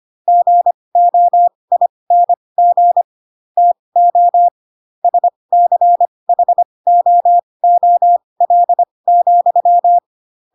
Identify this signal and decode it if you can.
Morse code, heard as a keyed tone, about 25 wpm, spaced as for 20 wpm, 700 Hz. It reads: GOING TO SCHOOL,
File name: Pinocchio / Pinocchio ch4 15 25 ik1hge